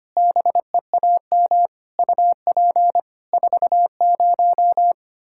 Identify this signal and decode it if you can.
Morse code, heard as a keyed tone, about 25 wpm, 700 Hz. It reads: BEAM UP 40
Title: Practice QSO - Exchange 2 - Info - 25wpm